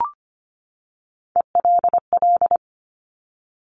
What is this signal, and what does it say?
Morse code, heard as a keyed tone, about 25 wpm, 700 Hz. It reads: ELL
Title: Top Three Letter Combinations in Words 25wpm